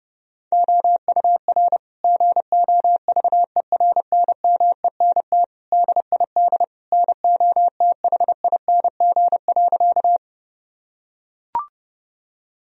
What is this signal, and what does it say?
Morse code, heard as a keyed tone, about 30 wpm, 700 Hz. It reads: OUR GOVERNMENT DID NOTHING.
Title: Sentences from Top 1000 Words - Review Part 3 - 30wpm